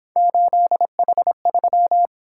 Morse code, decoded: 8H3

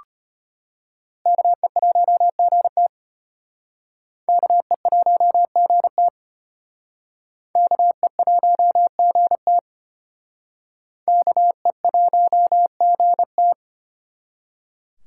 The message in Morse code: KE1GT KE1GT KE1GT KE1GT

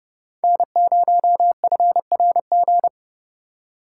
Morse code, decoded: N0FRG

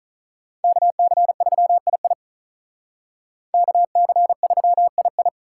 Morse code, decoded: KC3II KC3II

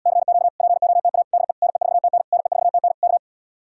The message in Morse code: CQ POTA DE NE4TN NE4TN K

